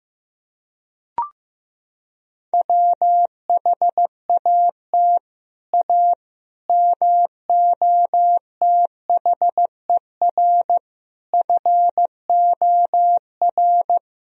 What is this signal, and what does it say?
Morse code, heard as a keyed tone, about 15 wpm, 700 Hz. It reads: WHAT A MOTHER FOR
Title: Sentences from Top 1000 Words - Review Part 1 - 15wpm